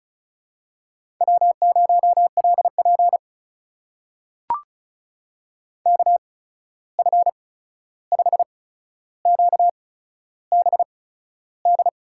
Morse code, heard as a keyed tone, about 36 wpm, spaced as for 13 wpm, 700 Hz. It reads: W0LP KF5QBD
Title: USA Call Signs - 8x Character Spacing - 35wpm